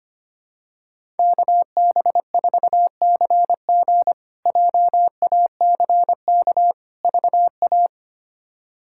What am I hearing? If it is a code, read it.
Morse code, heard as a keyed tone, about 25 wpm, 700 Hz. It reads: KB4CG JACK VA
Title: K1USN SST QSO Practice - 25wpm